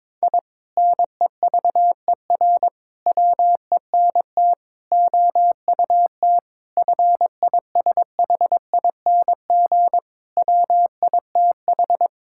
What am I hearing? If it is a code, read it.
Morse code, heard as a keyed tone, about 22 wpm, 700 Hz. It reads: I NEVER WENT OUT FISHING WITH
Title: Sentences from Top 1000 Words - Review Part 5 - 22wpm